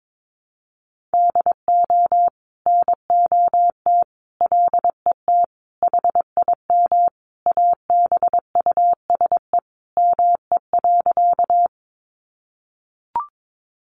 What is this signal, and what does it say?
Morse code, heard as a keyed tone, about 22 wpm, 700 Hz. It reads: DO NOT LET HIM ABUSE ME.
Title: Sentences from Top 1300 Words 22wpm